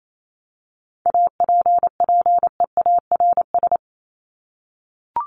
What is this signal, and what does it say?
Morse code, heard as a keyed tone, about 28 wpm, 700 Hz. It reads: APPEARS
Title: ES:S Suffix - Top 100 Words with Base Words 28wpm